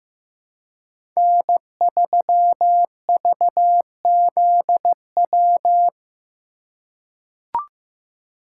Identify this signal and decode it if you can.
Morse code, heard as a keyed tone, about 15 wpm, 700 Hz. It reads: N3VZW
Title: USA Call Signs - 5 Characters - 3x Character Spacing - 15wpm